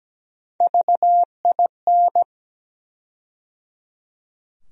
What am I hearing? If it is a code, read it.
Morse code, heard as a keyed tone, about 17 wpm, 700 Hz. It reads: VIN
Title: USA Call Sign Postfixes - 3 Letters 17wpm